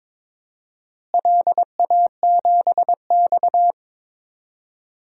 Morse code, heard as a keyed tone, about 22 wpm, 700 Hz. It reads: LA7X
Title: DXCC-100 Call Signs - 4 Characters - Repeated 2x - 22wpm